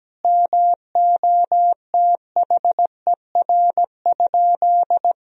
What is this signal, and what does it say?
Morse code, heard as a keyed tone, about 17 wpm, 700 Hz. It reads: MOTHER?
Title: Sentences from Top 1000 Words - Review Part 2 - 17wpm